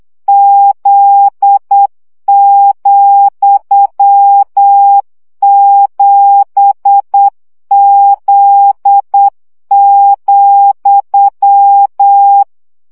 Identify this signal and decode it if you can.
Morse code, heard as a keyed tone, about 8 wpm, 800 Hz. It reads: Z,7Z,
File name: CW gyakorlas z 7 vesszo 7 baud